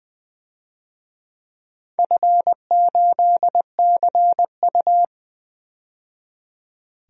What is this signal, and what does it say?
Morse code, heard as a keyed tone, about 20 wpm, 700 Hz. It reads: F8CU
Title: DXCC-100 Call Signs - 4 Characters - 20wpm